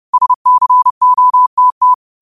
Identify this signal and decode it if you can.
Morse code, heard as a keyed tone, about 31 wpm, 1000 Hz. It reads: SCOTT
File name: qotdfiles / 30wpm